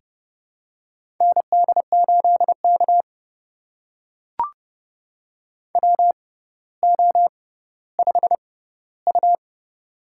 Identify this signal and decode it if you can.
Morse code, heard as a keyed tone, about 32 wpm, spaced as for 16 wpm, 700 Hz. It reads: ND8K WO5U
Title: USA Call Signs - 4 Characters - 6x Character Spacing - 30wpm